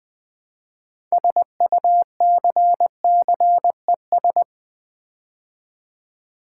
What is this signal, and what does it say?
Morse code, heard as a keyed tone, about 20 wpm, 700 Hz. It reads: SUCCES
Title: Build A Word - 3 Letters At A Time - 9-Letter Words - 20wpm